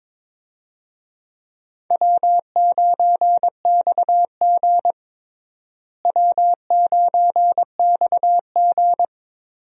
Morse code, heard as a keyed tone, about 22 wpm, 700 Hz. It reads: W9XG W9XG
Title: USA Call Signs - 4 Characters - Repeated 2x - 22wpm